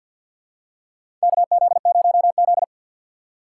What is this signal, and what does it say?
Morse code, heard as a keyed tone, about 53 wpm, 700 Hz. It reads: KZ0Z
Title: USA Call Signs - 6x Character Spacing - 50wpm